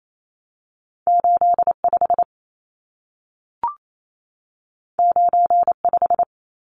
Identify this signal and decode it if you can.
Morse code, heard as a keyed tone, about 28 wpm, 700 Hz. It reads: 85 95
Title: Numbers - 2-Digit 28wpm